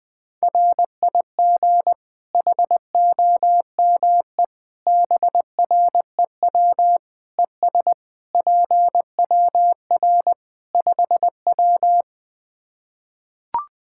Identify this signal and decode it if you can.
Morse code, heard as a keyed tone, about 20 wpm, 700 Hz. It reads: RIG HOME BREW ES PWR 5W
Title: K1USN SST QSO Practice 20wpm